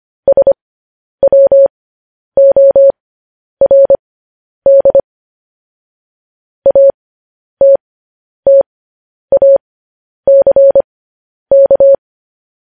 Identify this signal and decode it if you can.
Morse code, heard as a keyed tone, about 25 wpm, spaced as for 10 wpm, 550 Hz. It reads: SWORD ATTACK
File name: podcast / News headlines-1x-25x10